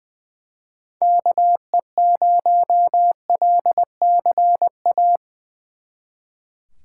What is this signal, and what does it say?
Morse code, heard as a keyed tone, about 23 wpm, 700 Hz. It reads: KE0LCA
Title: Speed Racing - USA Call Signs 20wpm